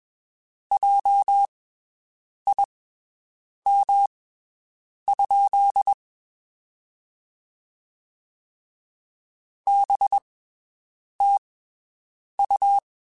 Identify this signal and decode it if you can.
Morse code, heard as a keyed tone, about 21 wpm, spaced as for 7 wpm, 800 Hz. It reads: JIM? BTU
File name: (spaced) / QSOs Spaced-mp3s / WD0BGP 1 @20 sp